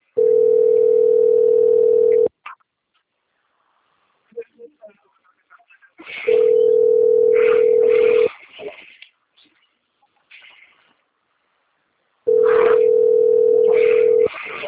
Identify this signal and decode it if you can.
Ringback tone (North America), 440 + 480 Hz, with speech.
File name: out- - - - - .2072